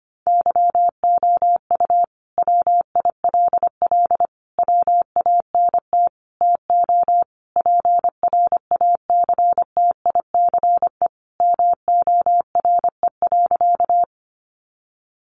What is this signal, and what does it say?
Morse code, heard as a keyed tone, about 25 wpm, 700 Hz. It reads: YOU WILL WANT TO PRACTICE MORE.
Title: Sentences from Top 700 Words 25wpm